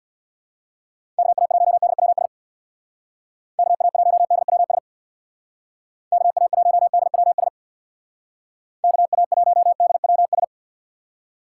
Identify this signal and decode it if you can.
Morse code, heard as a keyed tone, about 65 wpm, 700 Hz. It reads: KA1DWS KA1DWS KA1DWS KA1DWS